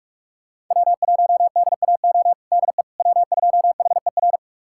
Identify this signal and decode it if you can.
Morse code, heard as a keyed tone, about 46 wpm, 700 Hz. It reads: W1DAO DE W2HER
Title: USA Callsign DE Callsign - 5 Characters - 45wpm